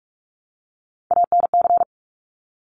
Morse code, heard as a keyed tone, about 46 wpm, 700 Hz. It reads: ANC
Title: Top Three Letter Combinations in Words 45wpm